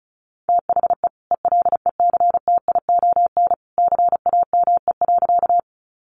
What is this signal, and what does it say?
Morse code, heard as a keyed tone, about 34 wpm, 700 Hz. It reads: THE ELECTION CAME.
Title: Sentences from Top 1500 Words 35wpm